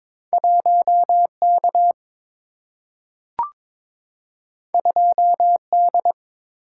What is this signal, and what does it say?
Morse code, heard as a keyed tone, about 23 wpm, 700 Hz. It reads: 1 K 2 D
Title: Single-Letter Single-Number 22wpm